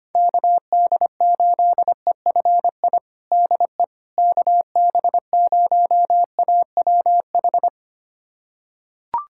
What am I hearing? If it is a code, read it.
Morse code, heard as a keyed tone, about 25 wpm, 700 Hz. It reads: KD8EFI DE KB0AWH
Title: USA Callsign DE Callsign - 6 Characters - 25wpm